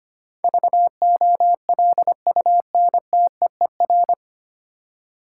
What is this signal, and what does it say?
Morse code, heard as a keyed tone, about 25 wpm, 700 Hz. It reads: VOLUNTEER
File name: practice / 25wpm / 9 Letter Words - First 3 Letters Given - 25wpm